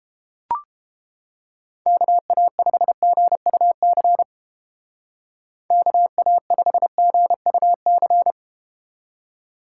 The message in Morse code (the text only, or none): KA5GUC KA5GUC